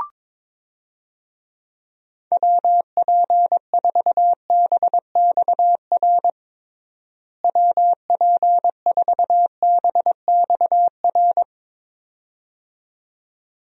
Morse code, heard as a keyed tone, about 22 wpm, 700 Hz. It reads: WP4BXR WP4BXR
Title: USA Call Signs - 6 Characters - Repeated 2x - 22wpm